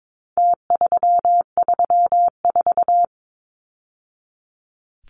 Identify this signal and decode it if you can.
Morse code, heard as a keyed tone, about 22 wpm, 700 Hz. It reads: T334
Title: Numbers - Contest Style - 4-Digit 22wpm